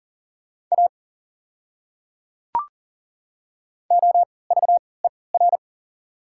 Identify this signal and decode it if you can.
Morse code, heard as a keyed tone, about 40 wpm, spaced as for 23 wpm, 700 Hz. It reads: A OVER
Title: 3x Character Spacing - Top 100 Words - 40wpm